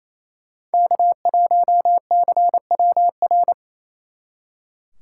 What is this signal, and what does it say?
Morse code, heard as a keyed tone, about 37 wpm, 700 Hz. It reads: K1CWR